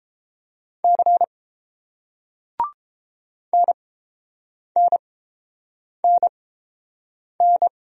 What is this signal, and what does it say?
Morse code, heard as a keyed tone, about 30 wpm, spaced as for 7 wpm, 700 Hz. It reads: C NNNN